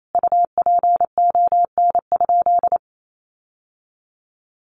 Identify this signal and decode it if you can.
Morse code, heard as a keyed tone, about 28 wpm, 700 Hz. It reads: UPON?
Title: Sentences from Top 1300 Words 28wpm